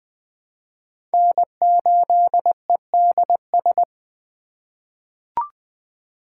Morse code, heard as a keyed tone, about 20 wpm, 700 Hz. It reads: N8EDS
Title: USA Call Signs - 5 Characters - 4x Character Spacing - 20wpm